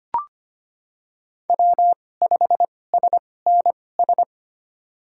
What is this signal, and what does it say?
Morse code, heard as a keyed tone, about 25 wpm, spaced as for 18 wpm, 700 Hz. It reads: W5SNS
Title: USA Call Signs - 5 Characters - 2x Character Spacing - 25wpm